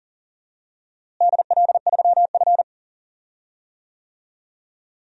Morse code, heard as a keyed tone, about 37 wpm, 700 Hz. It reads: DL3F
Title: DXCC-100 Call Signs - 4 Characters - Repeated 3x - 40wpm